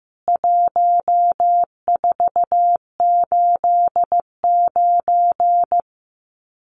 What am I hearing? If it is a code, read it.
Morse code, heard as a keyed tone, about 15 wpm, 700 Hz. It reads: 1489
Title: CWT Exchange 15wpm